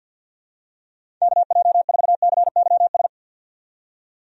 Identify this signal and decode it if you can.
Morse code, heard as a keyed tone, about 51 wpm, 700 Hz. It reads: KJ4CYS
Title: USA Call Signs - 6 Characters - 50wpm